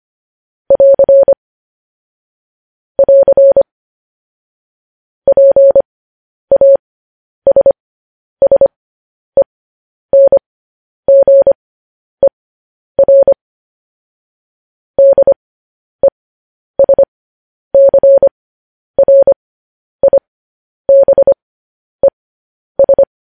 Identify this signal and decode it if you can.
Morse code, heard as a keyed tone, about 25 wpm, spaced as for 10 wpm, 550 Hz. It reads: <AR> <AR> PASSENGER DESCRIBES
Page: News Headlines in Morse Code at 10 WPM